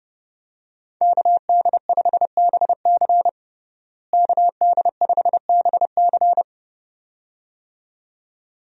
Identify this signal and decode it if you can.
Morse code, heard as a keyed tone, about 30 wpm, 700 Hz. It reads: KD5BC KD5BC
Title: USA Call Signs - 5 Characters - Repeated 2x - 30wpm